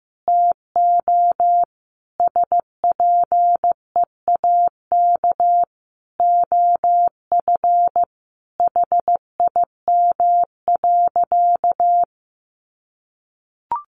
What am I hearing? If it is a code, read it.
Morse code, heard as a keyed tone, about 15 wpm, 700 Hz. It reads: TO SPEAK OF HIM.
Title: Sentences from Top 1100 Words 15wpm